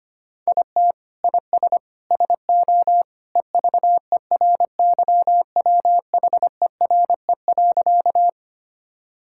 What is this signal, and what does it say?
Morse code, heard as a keyed tone, about 25 wpm, 700 Hz. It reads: IT IS SO EVERYWHERE.
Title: Sentences from Top 1600 Words 25wpm